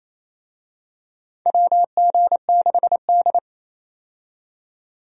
Morse code, heard as a keyed tone, about 28 wpm, 700 Hz. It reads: WG6D
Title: USA Call Signs - 4 Characters - 28wpm